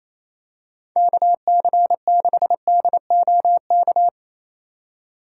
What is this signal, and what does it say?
Morse code, heard as a keyed tone, about 28 wpm, 700 Hz. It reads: KC6DOK
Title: USA Call Signs - 6 Characters - 2x Character Spacing - 28wpm